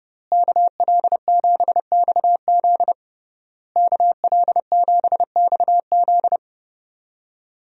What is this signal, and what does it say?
Morse code, heard as a keyed tone, about 30 wpm, 700 Hz. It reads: KL7XZ KL7XZ
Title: DXCC-100 Call Signs - 5 Characters - Repeated 3x - 30wpm